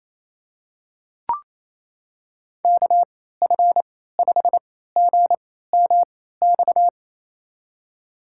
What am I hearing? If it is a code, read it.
Morse code, heard as a keyed tone, about 28 wpm, spaced as for 16 wpm, 700 Hz. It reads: KF5GMX
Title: USA Call Signs - 3x Character Spacing - 28wpm